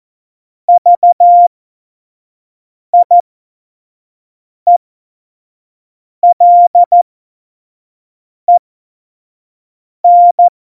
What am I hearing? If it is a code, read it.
Morse code, heard as a keyed tone, about 14 wpm, spaced as for 5 wpm, 700 Hz. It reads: VIELEN
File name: Morsekurs / Witze / Witz 19. Februar 2019, 25 F70